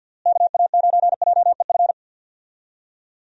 Morse code, heard as a keyed tone, about 51 wpm, 700 Hz. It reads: KA9JEF
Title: USA Call Signs - 6 Characters - Repeated 2x - 50wpm